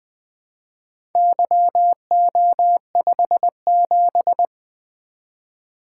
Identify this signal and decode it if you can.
Morse code, heard as a keyed tone, about 20 wpm, 700 Hz. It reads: YO57
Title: Four Letter-Number Combinations - 20wpm